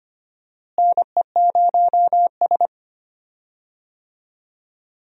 Morse code, heard as a keyed tone, about 25 wpm, 700 Hz. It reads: NE0S
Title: USA Call Signs - 4 Characters - Repeated 3x - 25wpm